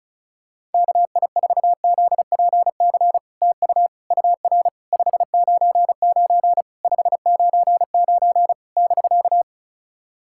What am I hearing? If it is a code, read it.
Morse code, heard as a keyed tone, about 36 wpm, 700 Hz. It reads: KI4ZPC TU UR 599 599 <BK>
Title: SOTA QSO Practice 35wpm